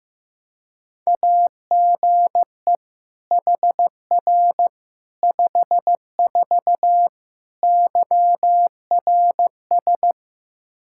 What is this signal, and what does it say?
Morse code, heard as a keyed tone, about 15 wpm, 700 Hz. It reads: AGE HR 54 YRS
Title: Practice QSO - Exchange 3 - More Info - 15wpm